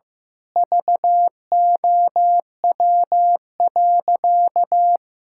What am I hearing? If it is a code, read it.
Morse code, heard as a keyed tone, about 15 wpm, 700 Hz. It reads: VOW.
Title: Pangram Challenge v2 - 15wpm